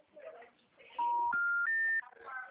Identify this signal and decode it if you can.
Special information tone.